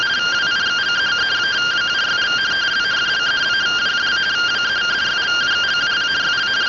Two-tone teleprinter signal, about 170 Hz shift.